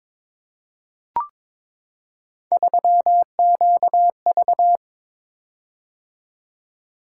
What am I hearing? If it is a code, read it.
Morse code, heard as a keyed tone, about 22 wpm, 700 Hz. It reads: 3QV
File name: Three Letter-Number Combinations - 22wpm